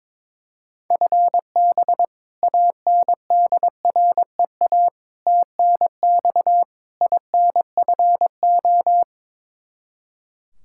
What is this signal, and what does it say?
Morse code, heard as a keyed tone, about 22 wpm, 700 Hz. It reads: FB ANDREA TNX INFO
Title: Practice QSO - Exchange 3 - More Info - 22wpm